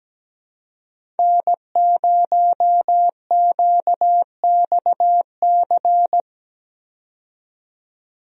Morse code, heard as a keyed tone, about 17 wpm, 700 Hz. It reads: N0QXC
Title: USA Call Signs - 5 Characters - 17wpm